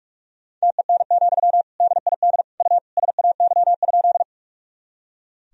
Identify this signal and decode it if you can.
Morse code, heard as a keyed tone, about 46 wpm, 700 Hz. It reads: TEN, DID U SAY?
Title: Sentences from Top 400 Words and Abbreviations 45wpm